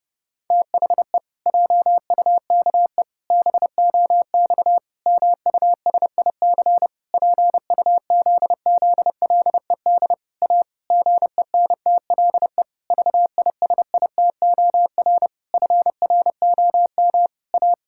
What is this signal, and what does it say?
Morse code, heard as a keyed tone, about 32 wpm, 700 Hz. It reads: THE JUKE BOX MUSIC PUZZLED A GENTLE VISITOR FROM A
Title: Pangram Challenge v2 - 30wpm